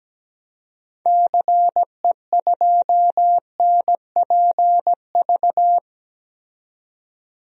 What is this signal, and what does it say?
Morse code, heard as a keyed tone, about 17 wpm, 700 Hz. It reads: CE2NPV
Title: DXCC-100 Call Signs - 6 Characters - 17wpm